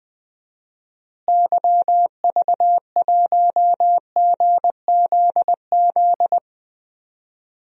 Morse code, heard as a keyed tone, about 20 wpm, 700 Hz. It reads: YV1GZZ
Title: DXCC-100 Call Signs - 6 Characters - Repeated 2x - 20wpm